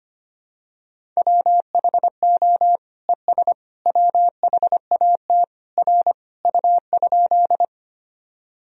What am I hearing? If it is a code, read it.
Morse code, heard as a keyed tone, about 25 wpm, 700 Hz. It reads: WHO ES WHAT R U?
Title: Sentences from Top 100 Words and Abbreviations 25wpm